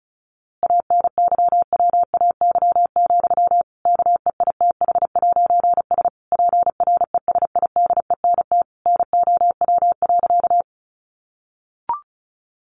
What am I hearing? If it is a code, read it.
Morse code, heard as a keyed tone, about 34 wpm, 700 Hz. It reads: ANYWAY, KEITH'S PRESIDENT NOW.
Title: A Sound Of Thunder 35wpm